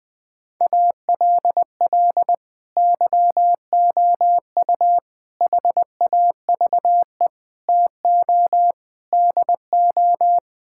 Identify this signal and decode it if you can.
Morse code, heard as a keyed tone, about 20 wpm, 700 Hz. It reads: ALL YOU HAVE TO DO
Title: Build A Phrase - 5-Word Phrases - 20wpm